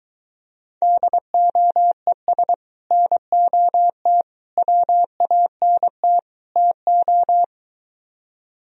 Morse code, heard as a keyed tone, about 23 wpm, 700 Hz. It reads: DOES NOT WANT TO